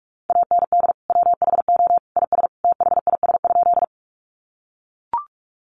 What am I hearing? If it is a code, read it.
Morse code, heard as a keyed tone, about 44 wpm, 700 Hz. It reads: AND WHO IS THIS?
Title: Sentences from Top 100 Words 45wpm